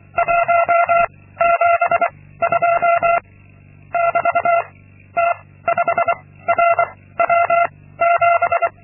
Morse code, heard as a keyed tone, about 25 wpm, 680 Hz. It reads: 1 7 2 <BT> T 5 R W 7
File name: Linksmakalnis-2